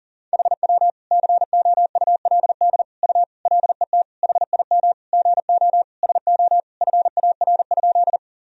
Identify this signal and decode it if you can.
Morse code, heard as a keyed tone, about 37 wpm, 700 Hz. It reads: HW COULD U LET HIM GO SO FAR?
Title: Sentences from Top 300 Words and Abbreviations 40wpm